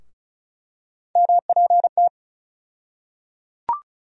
Morse code, heard as a keyed tone, about 34 wpm, 700 Hz. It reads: MPT
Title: USA Call Sign Postfixes - 3 Letters 35wpm